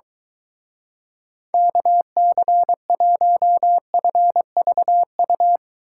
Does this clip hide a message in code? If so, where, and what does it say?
Yes: Morse code from 1.5 to 5.6 s, heard as a keyed tone, about 23 wpm, 700 Hz. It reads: KC1FVU